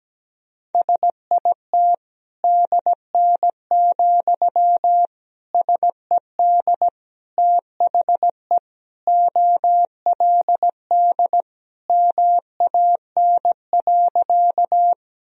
Morse code, heard as a keyed tone, about 17 wpm, 700 Hz. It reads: SIT DN, SED THE OLD MAN.
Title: Sentences from Top 500 Words and Abbreviations 17wpm